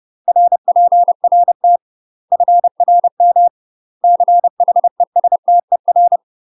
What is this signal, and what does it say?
Morse code, heard as a keyed tone, about 30 wpm, 700 Hz. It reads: RPRT FRM CHESTER
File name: UK MASTER QSOs TRAINING / 30WPM / UK MASTER QSOs 0014